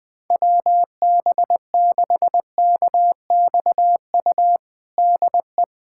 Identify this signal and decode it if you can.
Morse code, heard as a keyed tone, about 20 wpm, 700 Hz. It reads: WB6KXU DE
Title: USA Callsign DE Callsign 20wpm